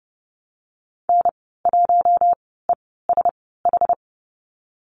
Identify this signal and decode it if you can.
Morse code, heard as a keyed tone, about 30 wpm, spaced as for 17 wpm, 700 Hz. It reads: N1ESH
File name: Lesson 14g - Number 2 - Call Signs 30wpm